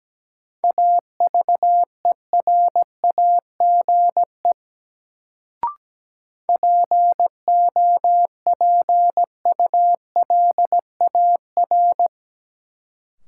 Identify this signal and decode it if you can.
Morse code, heard as a keyed tone, about 17 wpm, 700 Hz. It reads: AVERAGE POPULAR
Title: Sets of 5 Words - New Word Drill - 17wpm